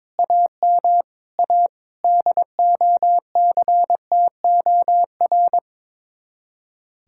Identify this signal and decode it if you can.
Morse code, heard as a keyed tone, about 23 wpm, 700 Hz. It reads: AM A DOCTOR
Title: Speed-Racing Common QSO Phrases Encore - 22wpm